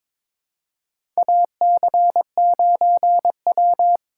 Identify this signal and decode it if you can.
Morse code, heard as a keyed tone, about 22 wpm, 700 Hz. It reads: AC9W
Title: USA Call Signs - 4 Characters - 2x Character Spacing - 22wpm